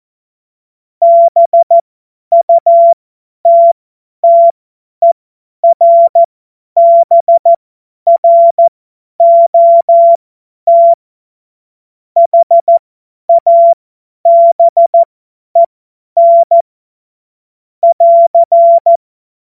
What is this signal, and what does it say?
Morse code, heard as a keyed tone, about 14 wpm, spaced as for 10 wpm, 680 Hz. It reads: BUTTERBROT HABEN <AR>
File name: Morsekurs / Witze / Witz 28. März 2023 50 F70